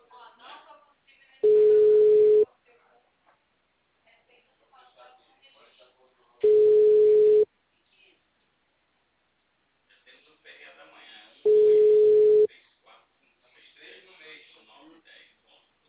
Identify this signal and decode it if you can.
Ringback tone (Europe, 425 Hz), with speech.